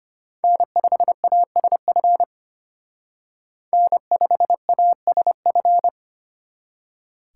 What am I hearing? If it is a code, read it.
Morse code, heard as a keyed tone, about 32 wpm, 700 Hz. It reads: N5ASF N5ASF